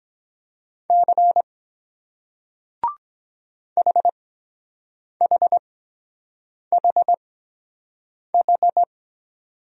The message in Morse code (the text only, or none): C HHHH